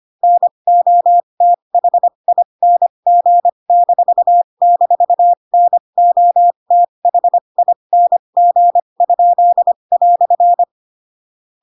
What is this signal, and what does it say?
Morse code, heard as a keyed tone, about 25 wpm, spaced as for 22 wpm, 700 Hz. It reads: NOTHING--NOTHING?"
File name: Pinocchio / Pinocchio ch7 18 25 ik1hge